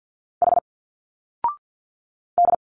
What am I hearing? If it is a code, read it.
Morse code, heard as a keyed tone, about 50 wpm, 700 Hz. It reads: H D
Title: Single Letters 50wpm